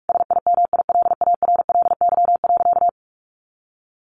Morse code, heard as a keyed tone, about 44 wpm, 700 Hz. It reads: SIMILARLY.
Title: Sentences from Top 1400 Words 45wpm